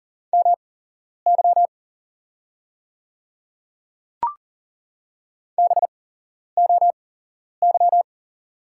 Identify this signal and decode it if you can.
Morse code, heard as a keyed tone, about 40 wpm, spaced as for 11 wpm, 690 Hz. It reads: MY BOY
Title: 8x Character Spacing - Top 200 Words - 40wpm